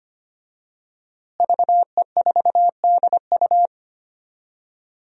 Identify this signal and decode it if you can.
Morse code, heard as a keyed tone, about 25 wpm, 700 Hz. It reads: VE4DU
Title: DXCC-100 Call Signs - 5 Characters - Repeated 2x - 25wpm